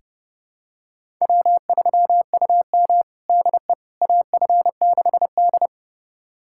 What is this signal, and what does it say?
Morse code, heard as a keyed tone, about 32 wpm, 700 Hz. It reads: W3UM DE AF6D
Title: USA Callsign DE Callsign - 4 Characters - 30wpm